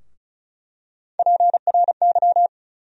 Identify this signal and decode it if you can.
Morse code, heard as a keyed tone, about 35 wpm, 700 Hz. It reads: PRY
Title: USA Call Sign Postfixes - 3 Letters 35wpm